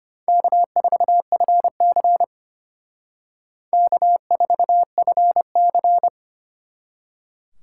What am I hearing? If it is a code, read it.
Morse code, heard as a keyed tone, about 32 wpm, 700 Hz. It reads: K4FC K4FC